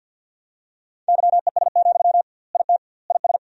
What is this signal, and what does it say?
Morse code, heard as a keyed tone, about 52 wpm, 700 Hz. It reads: YES, IT IS
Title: 2x Word-Spacing - Sentences from Top 500 Words - 50wpm